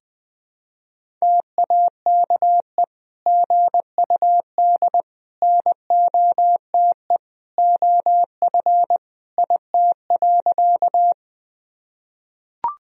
Morse code, heard as a keyed tone, about 20 wpm, 700 Hz. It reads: TAKE GUD NOTE OF IT.
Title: Sentences from Top 500 Words and Abbreviations 20wpm